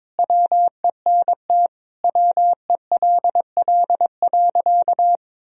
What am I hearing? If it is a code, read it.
Morse code, heard as a keyed tone, about 21 wpm, 700 Hz. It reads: WENT WELL.
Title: Sentences from Top 1000 Words - Review Part 4 - 22wpm